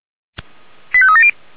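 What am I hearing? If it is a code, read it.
Five-tone selective call (ZVEI), heard as quick runs of five tones, tones of about 70 ms. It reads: 85370
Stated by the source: Five-tone ZVEI selective call received on SDR.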